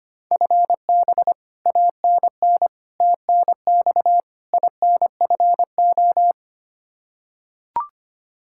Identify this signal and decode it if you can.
Morse code, heard as a keyed tone, about 25 wpm, 700 Hz. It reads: FB ANN TNX INFO
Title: Practice QSO - Exchange 3 - More Info - 25wpm